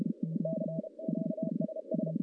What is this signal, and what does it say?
Morse code, heard as a keyed tone, about 32 wpm, 660 Hz, with noise. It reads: CQ SS
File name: contests / CW-SS-k5tr- -1200